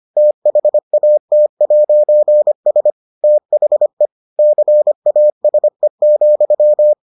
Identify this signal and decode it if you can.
Morse code, heard as a keyed tone, about 25 wpm, 600 Hz. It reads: THAT'S THE CASE,